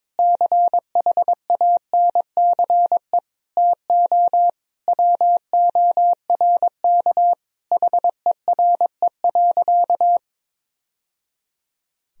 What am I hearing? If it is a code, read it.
Morse code, heard as a keyed tone, about 22 wpm, 700 Hz. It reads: CHANCE TO WORK HERE.
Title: Sentences from Top 1000 Words - Review Part 4 - 22wpm